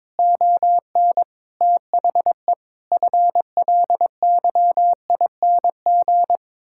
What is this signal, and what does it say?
Morse code, heard as a keyed tone, about 22 wpm, 700 Hz. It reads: ON THE FLYING
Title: Pangram Challenge v2 - 22wpm